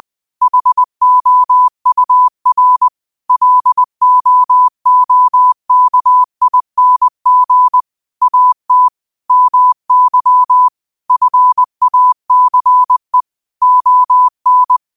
Morse code, heard as a keyed tone, about 20 wpm, 1000 Hz. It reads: HOUR LOOKING AT MY FACE ON